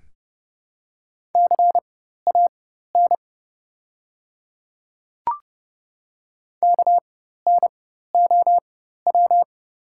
Morse code, heard as a keyed tone, about 30 wpm, spaced as for 14 wpm, 700 Hz. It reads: CAN KNOW
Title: 4x Character Spacing - Top 100 Words - 30wpm